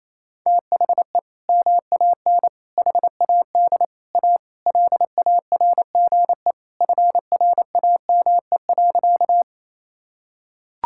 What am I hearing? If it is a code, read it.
Morse code, heard as a keyed tone, about 28 wpm, 700 Hz. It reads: THE MAN HAD A LARGE FRAME.
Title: Sentences from Top 1000 Words - Review Part 6 - 28wpm